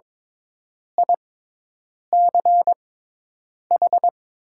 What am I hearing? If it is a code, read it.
Morse code, heard as a keyed tone, about 22 wpm, spaced as for 8 wpm, 700 Hz. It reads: ICH
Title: 6x Character Spacing - Top 100 Words - 22wpm